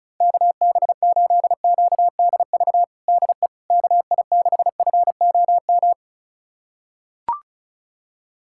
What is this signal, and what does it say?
Morse code, heard as a keyed tone, about 36 wpm, 700 Hz. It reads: KB8QDV DE KI6FOM
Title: USA Callsign DE Callsign - 6 Characters - 35wpm